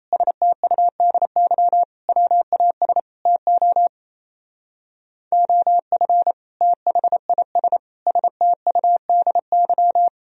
Morse code, heard as a keyed tone, about 33 wpm, 700 Hz. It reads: STUDY WAS TO OF THIS STUDY
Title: Speed-Racing - Sets of 5 Words 28wpm